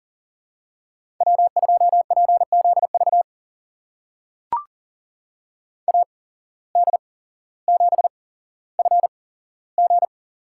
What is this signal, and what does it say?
Morse code, heard as a keyed tone, about 40 wpm, spaced as for 17 wpm, 700 Hz. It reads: W2PZV AD7FG
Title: USA Call Signs - 5 Characters - 8x Character Spacing - 40wpm